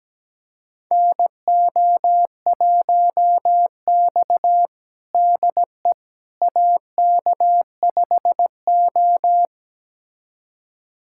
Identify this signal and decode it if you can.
Morse code, heard as a keyed tone, about 17 wpm, 700 Hz. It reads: NO1X DE AK5O
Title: USA Callsign DE Callsign - 4 Characters - 17wpm